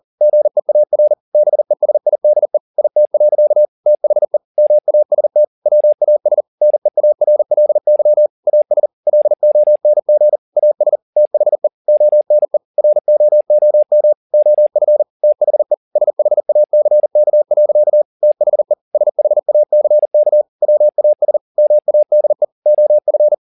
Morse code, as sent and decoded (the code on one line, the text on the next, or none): --. .- .-. / -... ... .. -.. / .. - .-.-.- / - .... / -- .- ... - / .-- .- ... / -. .- .-. .-.. -.-- / .- ... / .-.. --- -. --. / .- ... / - .... / --- -. / .-. --- --- -- / --- ..-. / - .... / ... .... .- -.-. -.- .-.-.- / - .... / ... .... .- -.-. -.- / .-- .- ... / -- .- -.. / --- ..-.
GEAR BESIDE IT. THE MAST WAS NEARLY AS LONG AS THE ONE ROOM OF THE SHACK. THE SHACK WAS MADE OF